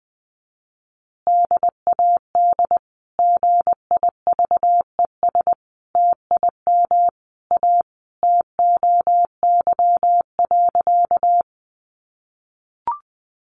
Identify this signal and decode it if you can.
Morse code, heard as a keyed tone, about 19 wpm, 700 Hz. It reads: DAD GIVES TIM A TOY.
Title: Easy Phrases 20wpm